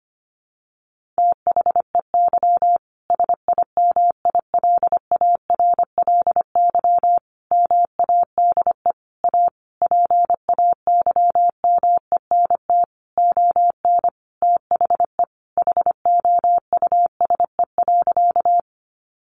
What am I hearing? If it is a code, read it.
Morse code, heard as a keyed tone, about 25 wpm, 700 Hz. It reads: THEY SIMILARLY MADE A PAYMENT ON THE HOUSE.
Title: Sentences from Top 1400 Words 25wpm